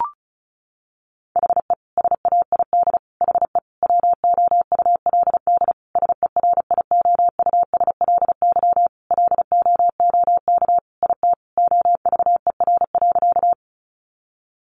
Morse code, heard as a keyed tone, about 34 wpm, 700 Hz. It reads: HE SAID HE WOULD SERIOUSLY LOOK IT OVER.
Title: Sentences from Top 1200 Words 35wpm